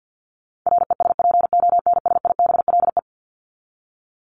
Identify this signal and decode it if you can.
Morse code, heard as a keyed tone, about 48 wpm, 700 Hz. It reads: RESPONSIBLE
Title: Top 751-1000 Words 50wpm